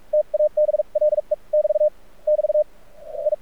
Morse code, heard as a keyed tone, about 46 wpm, spaced as for 38 wpm, 600 Hz. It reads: TABLE <BT> <BT> I
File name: archived / CRSnet